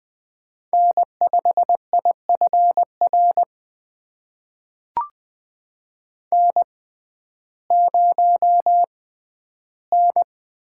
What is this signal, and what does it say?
Morse code, heard as a keyed tone, about 21 wpm, spaced as for 12 wpm, 700 Hz. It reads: N5IFR N0N
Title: USA Call Signs - 6x Character Spacing - 20wpm